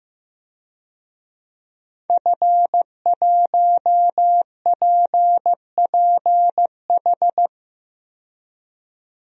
Morse code, heard as a keyed tone, about 16 wpm, 700 Hz. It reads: F1PPH
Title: DXCC-100 Call Signs - 5 Characters - 15wpm